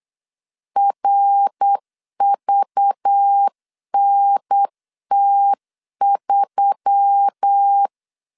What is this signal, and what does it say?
Morse code, heard as a keyed tone, about 8 wpm, 800 Hz. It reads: RVNT3
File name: LEZIONI / 7 GF9Z / 7 G 8wpm 3118